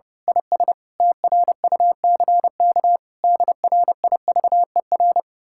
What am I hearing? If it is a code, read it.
Morse code, heard as a keyed tone, about 28 wpm, 700 Hz. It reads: IS TRUCK DRIVER
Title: Speed-Racing Common QSO Phrases Encore - 30wpm